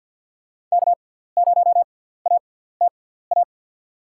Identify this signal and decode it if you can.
Morse code, heard as a keyed tone, about 52 wpm, spaced as for 17 wpm, 700 Hz. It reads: K0ATA
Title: USA Call Signs - 6x Character Spacing - 50wpm